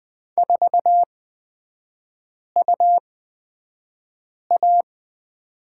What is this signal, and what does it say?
Morse code, heard as a keyed tone, about 20 wpm, spaced as for 5 wpm, 700 Hz. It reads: 4UA
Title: 250 Lesson 40e - BK - Three Characters 20wpm